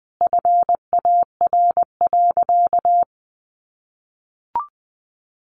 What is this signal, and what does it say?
Morse code, heard as a keyed tone, about 20 wpm, 700 Hz. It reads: FAR.
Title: Sentences from Top 1500 Words 20wpm